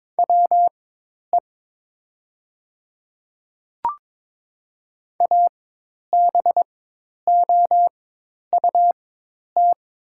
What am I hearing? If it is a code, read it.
Morse code, heard as a keyed tone, about 22 wpm, spaced as for 10 wpm, 700 Hz. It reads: WE ABOUT
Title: 4x Character Spacing - Top 100 Words - 22wpm